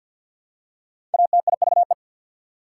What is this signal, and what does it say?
Morse code, heard as a keyed tone, about 54 wpm, 700 Hz. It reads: ATIVE
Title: Prefixes and Suffixes 50wpm